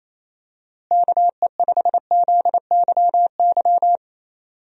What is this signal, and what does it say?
Morse code, heard as a keyed tone, about 28 wpm, 700 Hz. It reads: KE5ZYY
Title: USA Call Signs - 6 Characters - Repeated 2x - 28wpm